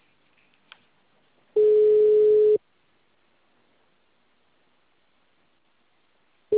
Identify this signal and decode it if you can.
Ringback tone (Europe, 425 Hz).
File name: out- -2039- - - .74642